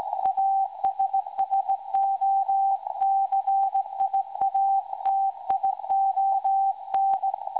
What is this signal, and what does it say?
Morse code, heard as a keyed tone, about 18 wpm, 780 Hz. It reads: ASSOCIATION
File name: WebSDR records / SAQ recording 17.2kHz